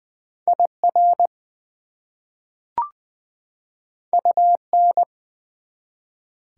Morse code, heard as a keyed tone, about 20 wpm, 700 Hz. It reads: IR UN
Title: Prefixes and Suffixes 20wpm